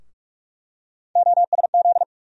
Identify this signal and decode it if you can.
Morse code, heard as a keyed tone, about 42 wpm, 700 Hz. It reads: OSZ
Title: USA Call Sign Postfixes - 3 Letters 45wpm